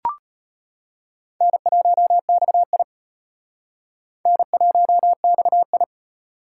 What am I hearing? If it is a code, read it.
Morse code, heard as a keyed tone, about 32 wpm, 700 Hz. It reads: N1XI N1XI